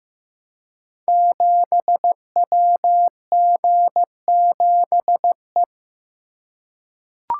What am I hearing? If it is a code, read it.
Morse code, heard as a keyed tone, about 15 wpm, 700 Hz. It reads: 7WG7E
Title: Code Groups - 5 Letter:Numbers - 15wpm